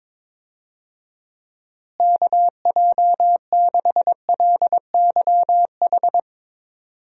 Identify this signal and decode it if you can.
Morse code, heard as a keyed tone, about 22 wpm, 700 Hz. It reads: KJ6LYH
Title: USA Call Signs - 6 Characters - Repeated 2x - 22wpm